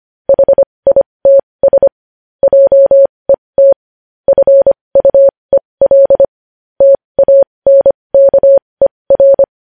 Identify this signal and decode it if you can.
Morse code, heard as a keyed tone, about 25 wpm, spaced as for 20 wpm, 550 Hz. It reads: HITS JET FUEL TANKER
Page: News Headlines in Morse Code at 20 WPM